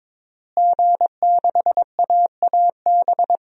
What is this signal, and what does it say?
Morse code, heard as a keyed tone, about 22 wpm, 700 Hz. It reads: G6AAB
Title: DXCC-100 Call Signs - 5 Characters - Repeated 2x - 22wpm